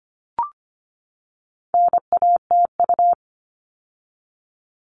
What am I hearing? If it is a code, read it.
Morse code, heard as a keyed tone, about 25 wpm, 700 Hz. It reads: NATU
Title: All Cut Numbers - 4-Digit 25wpm